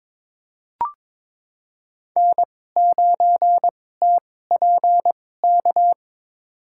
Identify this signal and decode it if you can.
Morse code, heard as a keyed tone, about 22 wpm, spaced as for 16 wpm, 700 Hz. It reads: N9TPK
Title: USA Call Signs - 2x Character Spacing - 22wpm